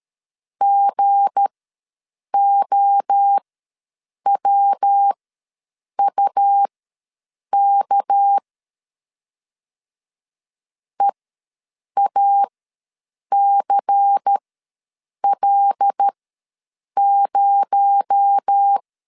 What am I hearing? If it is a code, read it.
Morse code, heard as a keyed tone, about 13 wpm, spaced as for 7 wpm, 800 Hz. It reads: GOWUK EACL0
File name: LEZIONI / 8 YJ45 / 8 YJ45 ws 12wpm 3146